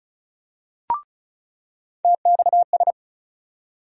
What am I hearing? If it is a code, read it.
Morse code, heard as a keyed tone, about 36 wpm, 700 Hz. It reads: TXS
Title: USA Call Sign Postfixes - 3 Letters 35wpm